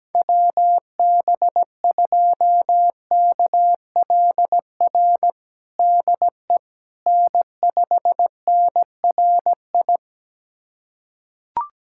Morse code, heard as a keyed tone, about 17 wpm, 700 Hz. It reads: WB2KLR DE N5NRI
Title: USA Callsign DE Callsign 17wpm